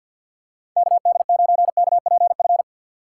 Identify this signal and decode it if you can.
Morse code, heard as a keyed tone, about 52 wpm, 700 Hz. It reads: KD9KPF
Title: USA Call Signs - 6 Characters - 3x Character Spacing - 50wpm